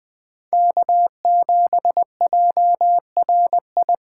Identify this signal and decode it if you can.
Morse code, heard as a keyed tone, about 24 wpm, 700 Hz. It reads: K7JRI
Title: Speed Racing - USA 1x3 Call Signs 20wpm